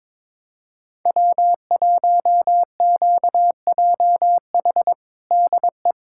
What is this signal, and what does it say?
Morse code, heard as a keyed tone, about 22 wpm, 700 Hz. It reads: W1QJH DE
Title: USA Callsign DE Callsign 22wpm